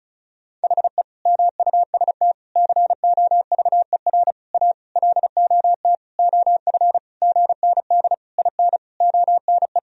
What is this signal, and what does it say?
Morse code, heard as a keyed tone, about 36 wpm, 700 Hz. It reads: HE MUST COVER A LOT OF GND IN ONE
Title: Sentences from Top 500 Words and Abbreviations 35wpm